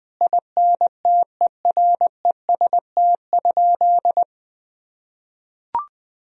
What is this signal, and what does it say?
Morse code, heard as a keyed tone, about 20 wpm, 700 Hz. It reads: INTEREST?
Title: Sentences from Top 1000 Words - Review Part 2 - 20wpm